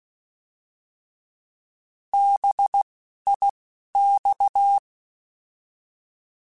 Morse code, heard as a keyed tone, about 16 wpm, spaced as for 11 wpm, 800 Hz. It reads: BIX
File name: Alphabetical / Alphabetical mp3s / Bs @10